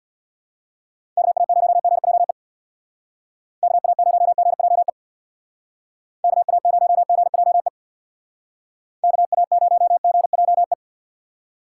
Morse code, heard as a keyed tone, about 62 wpm, 700 Hz. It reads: KA0GJE KA0GJE KA0GJE KA0GJE
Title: Speed Racing - USA Call Signs 50wpm